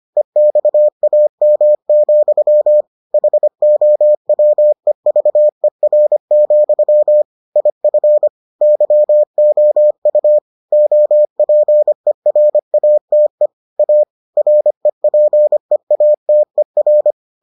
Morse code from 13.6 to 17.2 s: A REPEATER

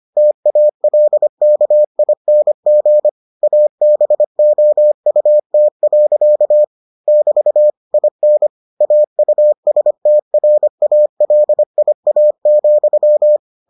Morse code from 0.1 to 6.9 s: TALKING ABOUT.

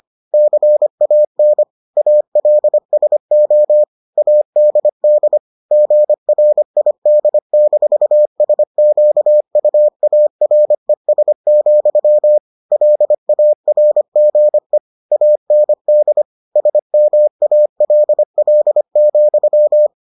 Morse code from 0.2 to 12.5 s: CAN ALSO ADD GRID-SQUARES,

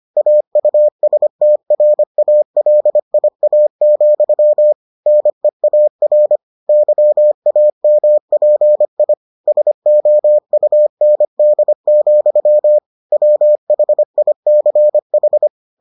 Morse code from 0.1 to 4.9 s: AUSTRALIA,

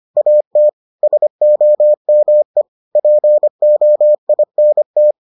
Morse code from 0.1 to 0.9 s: AT

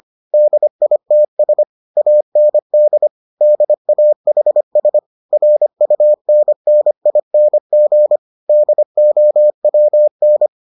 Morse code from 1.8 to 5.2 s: AND DAHS

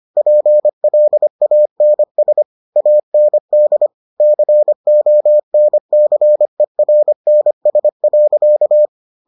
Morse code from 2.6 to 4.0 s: AND

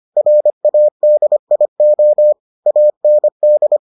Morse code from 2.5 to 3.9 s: AND

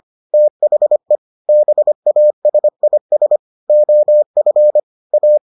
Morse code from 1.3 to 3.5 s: BASIS

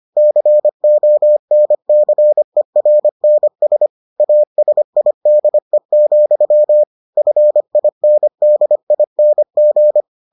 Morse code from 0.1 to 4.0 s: CONCERNS

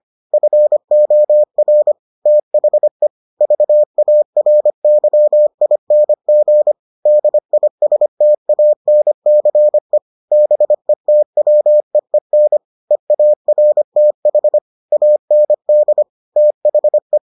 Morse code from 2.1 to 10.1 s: THE VARYING DISTANCE

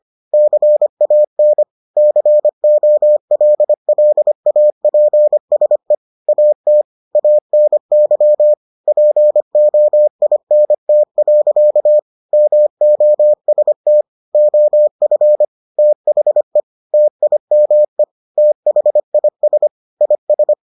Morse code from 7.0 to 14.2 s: ANY POINT. MOST

